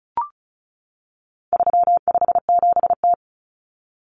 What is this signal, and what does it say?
Morse code, heard as a keyed tone, about 34 wpm, 700 Hz. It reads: 357T
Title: Common Cut Numbers - 4-Digit 35wpm